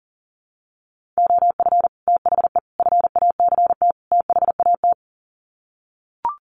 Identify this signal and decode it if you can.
Morse code, heard as a keyed tone, about 40 wpm, 700 Hz. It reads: OF THE FACT THAT
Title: Sets of 4 Words 40wpm